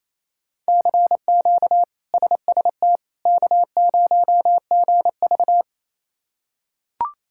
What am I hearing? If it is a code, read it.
Morse code, heard as a keyed tone, about 28 wpm, 700 Hz. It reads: CQ SST K0GV
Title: K1USN SST QSO Practice - 28wpm